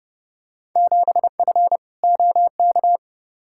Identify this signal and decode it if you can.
Morse code, heard as a keyed tone, about 31 wpm, 700 Hz. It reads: 7F OK
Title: Field Day Exchanges Encore - 30wpm